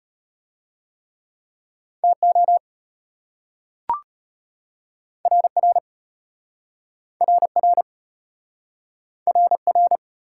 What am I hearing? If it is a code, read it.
Morse code, heard as a keyed tone, about 32 wpm, 700 Hz. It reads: TO RR RR RR